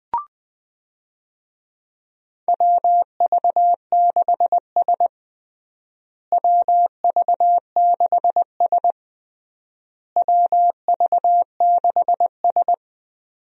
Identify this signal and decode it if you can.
Morse code, heard as a keyed tone, about 21 wpm, 700 Hz. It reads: WV6S WV6S WV6S
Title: USA Call Signs - 4 Characters - Repeated 3x - 20wpm